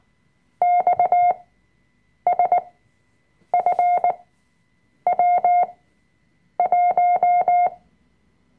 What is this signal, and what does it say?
Morse code, heard as a keyed tone, about 19 wpm, spaced as for 7 wpm, 710 Hz. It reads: XSFW1
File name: Workshop Dateien / CW-Online-Kurs / Lektion 005